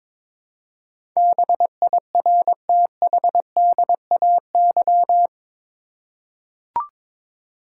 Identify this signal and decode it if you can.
Morse code, heard as a keyed tone, about 22 wpm, 700 Hz. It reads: BIRTHDAY
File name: practice / 22wpm / Eight Letter Words - 22wpm